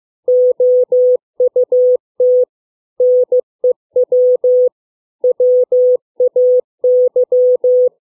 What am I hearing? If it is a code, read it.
Morse code, heard as a keyed tone, about 15 wpm, 500 Hz. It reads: OUT NEW WAY
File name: CW / fixed length words 4